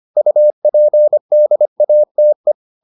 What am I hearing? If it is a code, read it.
Morse code, heard as a keyed tone, about 25 wpm, 600 Hz. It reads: UPDATE